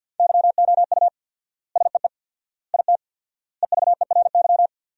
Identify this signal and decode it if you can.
Morse code, heard as a keyed tone, about 49 wpm, 700 Hz. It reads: YOU SEE IT EVERY
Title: 4x Word-Spacing - Sentences from Top 200 Words - 50wpm